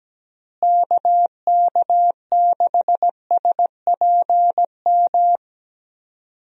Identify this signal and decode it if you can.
Morse code, heard as a keyed tone, about 17 wpm, 700 Hz. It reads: KK6SPM
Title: USA Call Signs - 6 Characters - Repeated 2x - 17wpm